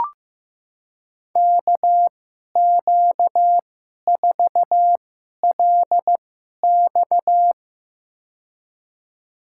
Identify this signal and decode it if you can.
Morse code, heard as a keyed tone, about 15 wpm, spaced as for 11 wpm, 700 Hz. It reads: KQ4LX
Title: USA Call Signs - 5 Characters - 2x Character Spacing - 15wpm